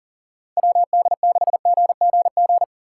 Morse code, heard as a keyed tone, about 40 wpm, 700 Hz. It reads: WD6GGG
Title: USA Call Signs - 6 Characters - 6x Character Spacing - 40wpm